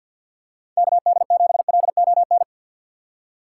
Morse code, heard as a keyed tone, about 51 wpm, 700 Hz. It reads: KD7LON
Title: USA Call Signs - 6 Characters - 6x Character Spacing - 50wpm